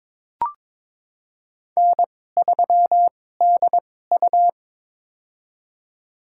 Morse code, heard as a keyed tone, about 21 wpm, spaced as for 16 wpm, 700 Hz. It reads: N3DU
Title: USA Call Signs - 4 Characters - 2x Character Spacing - 22wpm